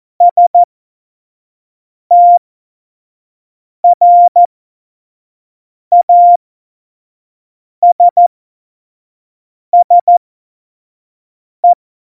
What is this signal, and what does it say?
Morse code, heard as a keyed tone, about 14 wpm, spaced as for 5 wpm, 700 Hz. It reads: STRASSE
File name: Morsekurs / Witze / Witz 17 April 2018, 25 F70